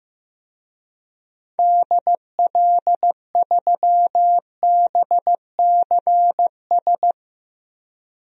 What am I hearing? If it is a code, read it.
Morse code, heard as a keyed tone, about 15 wpm, 700 Hz. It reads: DL3BCS
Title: DXCC-100 Call Signs - 6 Characters - 15wpm